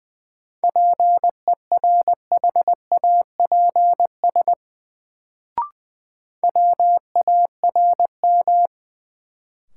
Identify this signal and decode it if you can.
Morse code, heard as a keyed tone, about 20 wpm, 700 Hz. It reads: PERHAPS WARM
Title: Sentences from Top 500 Words - New Word Drill - 20wpm